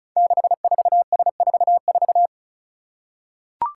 Morse code, heard as a keyed tone, about 36 wpm, 700 Hz. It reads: 64S44
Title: Five Challenging Letter-Number Combinations 35wpm